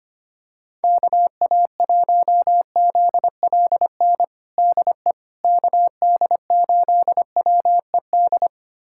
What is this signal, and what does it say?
Morse code, heard as a keyed tone, about 25 wpm, 700 Hz. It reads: KA1ZLN DE KD8WED
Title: USA Callsign DE Callsign - 6 Characters - 25wpm